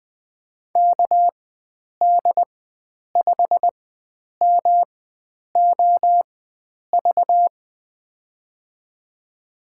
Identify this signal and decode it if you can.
Morse code, heard as a keyed tone, about 21 wpm, spaced as for 9 wpm, 700 Hz. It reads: KD5MOV
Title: USA Call Signs - 6 Characters - 4x Character Spacing - 20wpm